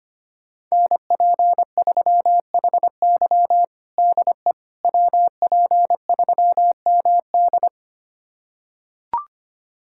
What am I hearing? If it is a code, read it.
Morse code, heard as a keyed tone, about 25 wpm, 700 Hz. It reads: NP3HY DE WP3MD
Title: USA Callsign DE Callsign - 5 Characters - 25wpm